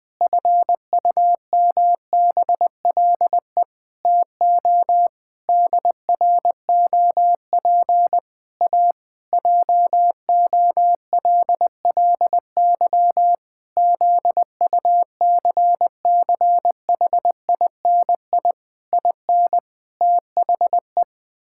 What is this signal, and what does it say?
Morse code, heard as a keyed tone, about 19 wpm, 700 Hz. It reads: FUMBLE TO DROP A JOLLY ZUCCHINI IN THE
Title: Pangram Challenge v2 - 20wpm